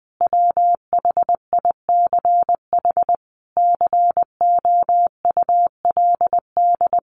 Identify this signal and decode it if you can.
Morse code, heard as a keyed tone, about 20 wpm, 700 Hz. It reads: WHICH COULD
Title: A Sound Of Thunder 20wpm